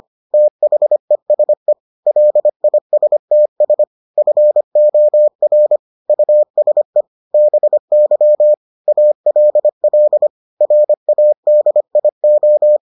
Morse code, heard as a keyed tone, about 25 wpm, 600 Hz. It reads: THESE LISTS FOR USE BY ALL RADIO